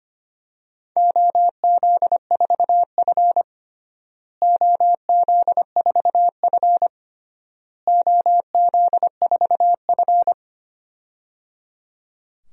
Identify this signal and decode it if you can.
Morse code, heard as a keyed tone, about 25 wpm, 700 Hz. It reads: OZ4F OZ4F OZ4F
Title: DXCC-100 Call Signs - 4 Characters - Repeated 3x - 25wpm